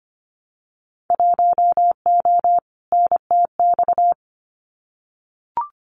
Morse code, heard as a keyed tone, about 25 wpm, 700 Hz. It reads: 1O NTX
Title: Winter Field Day Exchanges 25wpm